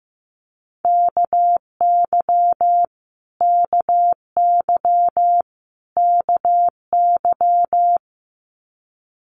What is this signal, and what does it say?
Morse code, heard as a keyed tone, about 15 wpm, 700 Hz. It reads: KY KY KY
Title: US States Repeated 3x 15wpm